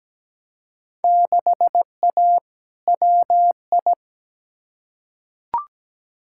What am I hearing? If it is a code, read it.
Morse code, heard as a keyed tone, about 17 wpm, 700 Hz. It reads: 6A WI
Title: Field Day Exchanges - 17wpm